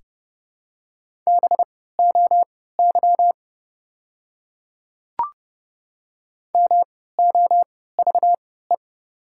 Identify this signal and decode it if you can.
Morse code, heard as a keyed tone, about 30 wpm, spaced as for 17 wpm, 700 Hz. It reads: BOY MOVE
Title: 3x Character Spacing - Top 200 Words - 30wpm